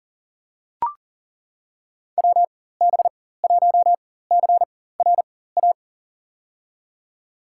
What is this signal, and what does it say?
Morse code, heard as a keyed tone, about 40 wpm, spaced as for 19 wpm, 700 Hz. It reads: WB1CRA
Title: USA Call Signs - 6 Characters - 4x Character Spacing - 40wpm